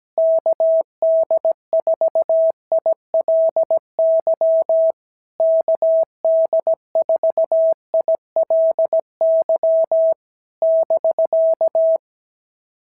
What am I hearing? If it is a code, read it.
Morse code, heard as a keyed tone, about 16 wpm, 650 Hz. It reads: KD4ILY KD4ILY <BK>
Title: SOTA QSO Practice 17wpm